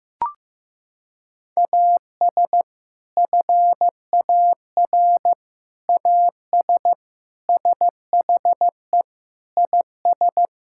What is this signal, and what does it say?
Morse code, heard as a keyed tone, about 15 wpm, 700 Hz. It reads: AS FAR AS SHE IS
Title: Sentences from Top 1000 Words - Review Part 6 - 15wpm